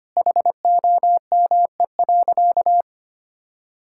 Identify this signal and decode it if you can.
Morse code, heard as a keyed tone, about 25 wpm, 700 Hz. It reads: HOME.
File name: practice / 25wpm / Sentences with 4-Word Phrases - 25wpm